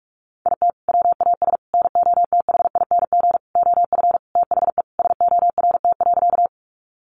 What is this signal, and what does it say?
Morse code, heard as a keyed tone, about 44 wpm, 700 Hz. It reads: IT WAS NOTHING OF THE SORT.
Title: Sentences from Top 1300 Words 45wpm